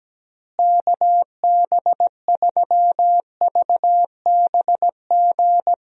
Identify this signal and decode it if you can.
Morse code, heard as a keyed tone, about 17 wpm, 700 Hz. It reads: KB3VBG
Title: USA Call Signs - 6 Characters - Repeated 3x - 17wpm